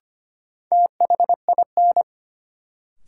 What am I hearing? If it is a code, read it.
Morse code, heard as a keyed tone, about 25 wpm, 700 Hz. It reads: THIN
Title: Sentences from Top 1200 Words - New Word Drill Alt - 25wpm